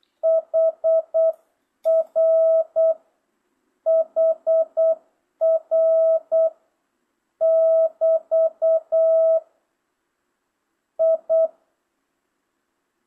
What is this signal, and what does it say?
Morse code, heard as a keyed tone, about 8 wpm, 660 Hz. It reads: HR HR <BT> I
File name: jh8cbh / 2025cw